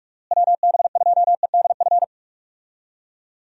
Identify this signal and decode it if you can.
Morse code, heard as a keyed tone, about 46 wpm, 700 Hz. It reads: WB2EDF
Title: USA Call Signs - 6 Characters - Repeated 2x - 45wpm